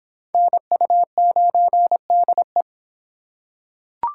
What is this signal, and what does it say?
Morse code, heard as a keyed tone, about 25 wpm, 700 Hz. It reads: NU9DE